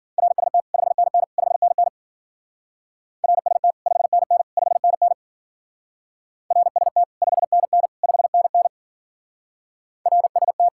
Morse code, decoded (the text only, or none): RST 5NN 5NN RST 5NN 5NN RST 5NN 5NN RST